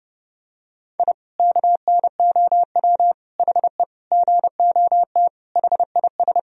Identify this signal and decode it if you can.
Morse code, heard as a keyed tone, about 29 wpm, 700 Hz. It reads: I KNOW HE GOT HIS
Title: Sentences from Top 1000 Words - Review Part 3 - 30wpm